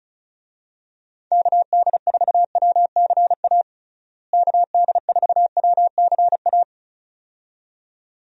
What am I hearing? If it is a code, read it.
Morse code, heard as a keyed tone, about 34 wpm, 700 Hz. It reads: KD4WCA KD4WCA
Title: USA Call Signs - 6 Characters - Repeated 2x - 35wpm